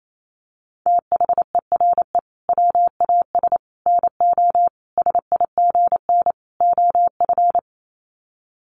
Morse code, heard as a keyed tone, about 28 wpm, 700 Hz. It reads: THERE WAS NO SIGN OF
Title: Sets of 5 Words Encore2 28wpm